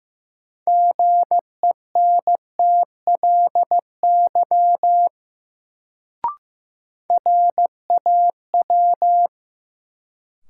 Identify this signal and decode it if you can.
Morse code, heard as a keyed tone, about 15 wpm, 700 Hz. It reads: GENTLY RAW
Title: Sentences from Top 1500 Words - New Word Drill - 15wpm